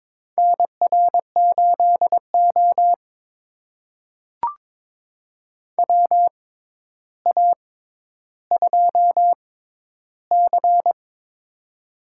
Morse code, heard as a keyed tone, about 23 wpm, spaced as for 11 wpm, 700 Hz. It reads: NR8O WA2C
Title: USA Call Signs - 4 Characters - 6x Character Spacing - 22wpm